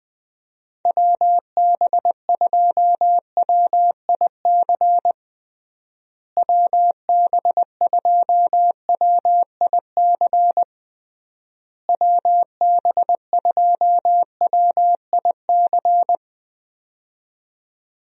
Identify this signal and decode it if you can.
Morse code, heard as a keyed tone, about 21 wpm, 700 Hz. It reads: WB2WIC WB2WIC WB2WIC
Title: USA Call Signs - 6 Characters - Repeated 3x - 20wpm